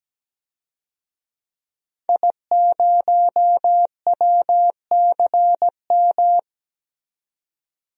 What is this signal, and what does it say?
Morse code, heard as a keyed tone, about 17 wpm, 700 Hz. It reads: I0WCM
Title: DXCC-100 Call Signs - 5 Characters - 17wpm